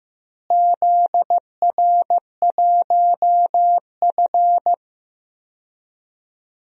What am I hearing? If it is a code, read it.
Morse code, heard as a keyed tone, about 15 wpm, 700 Hz. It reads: ZR1F
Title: DXCC-100 Call Signs - 4 Characters - Repeated 2x - 15wpm